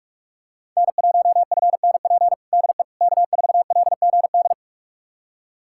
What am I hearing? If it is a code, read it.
Morse code, heard as a keyed tone, about 46 wpm, 700 Hz. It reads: N1FNP DE K4LGD
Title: USA Callsign DE Callsign - 5 Characters - 45wpm